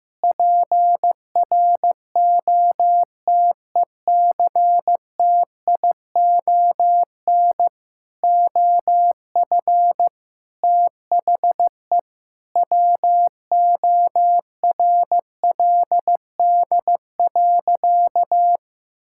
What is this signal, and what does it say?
Morse code, heard as a keyed tone, about 15 wpm, 700 Hz. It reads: PROTECTION OF THE WORLD.
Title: Sentences from Top 1600 Words 15wpm